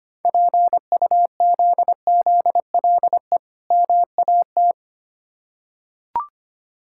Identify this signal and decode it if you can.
Morse code, heard as a keyed tone, about 25 wpm, 690 Hz. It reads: PUZZLE MAT
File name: practice / 25wpm / Difficult Words with Hints - 25wpm